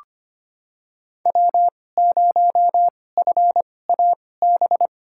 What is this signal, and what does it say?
Morse code, heard as a keyed tone, about 25 wpm, spaced as for 18 wpm, 700 Hz. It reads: W0FAB
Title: USA Call Signs - 5 Characters - 2x Character Spacing - 25wpm